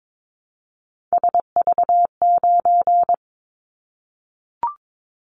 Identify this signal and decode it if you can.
Morse code, heard as a keyed tone, about 22 wpm, 700 Hz. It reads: SV9
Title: Callsign Prefixes for Top 100 DXCC - 22wpm